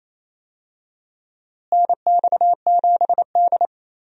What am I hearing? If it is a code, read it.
Morse code, heard as a keyed tone, about 28 wpm, 700 Hz. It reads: NX7D
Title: USA Call Signs - 4 Characters - Repeated 2x - 28wpm